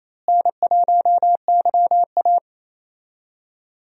Morse code, heard as a keyed tone, about 28 wpm, 700 Hz. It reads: N1YA
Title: USA Call Signs - 4 Characters - Repeated 3x - 28wpm